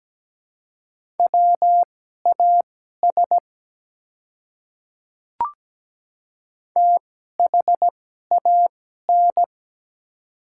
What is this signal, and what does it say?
Morse code, heard as a keyed tone, about 17 wpm, spaced as for 12 wpm, 700 Hz. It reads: WAS THAN
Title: 2x Character Spacing - Top 200 Words - 17wpm